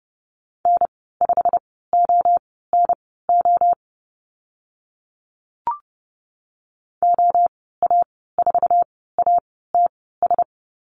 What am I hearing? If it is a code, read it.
Morse code, heard as a keyed tone, about 30 wpm, spaced as for 17 wpm, 700 Hz. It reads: N5ONO OA4ATS
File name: Lesson 15g - Number 5 - Call Signs 30wpm